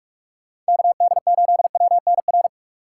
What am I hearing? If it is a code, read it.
Morse code, heard as a keyed tone, about 46 wpm, 700 Hz. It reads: KD8WNR
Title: USA Call Signs - 6 Characters - 3x Character Spacing - 45wpm